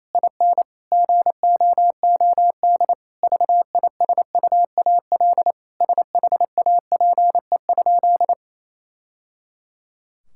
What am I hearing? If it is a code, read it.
Morse code, heard as a keyed tone, about 28 wpm, 700 Hz. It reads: IN GOOD VISUAL SHAPE?
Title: Sentences from Top 1000 Words - Review Part 6 - 28wpm